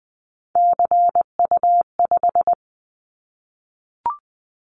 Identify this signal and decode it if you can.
Morse code, heard as a keyed tone, about 20 wpm, 700 Hz. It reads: CU5
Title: Callsign Prefixes for Top 50 DXCC - 20wpm